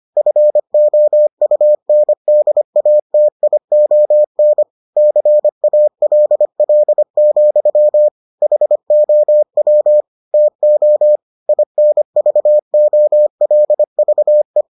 Morse code, heard as a keyed tone, about 25 wpm, 600 Hz. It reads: FOUNDATION CALL, HOW TO INVOLVE